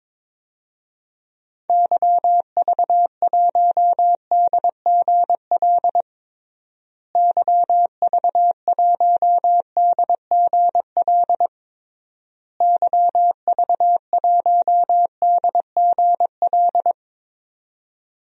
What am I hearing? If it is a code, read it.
Morse code, heard as a keyed tone, about 22 wpm, 700 Hz. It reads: YV1DGL YV1DGL YV1DGL
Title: DXCC-100 Call Signs - 6 Characters - Repeated 3x - 22wpm